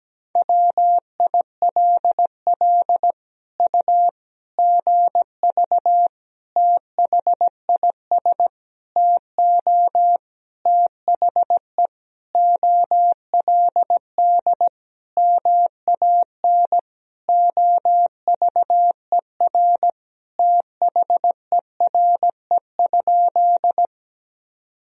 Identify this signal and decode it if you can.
Morse code, heard as a keyed tone, about 17 wpm, 700 Hz. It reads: WILL U GV THIS TO THE OLD MAN OVER THERE?
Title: Sentences from Top 200 Words and Abbreviations 17wpm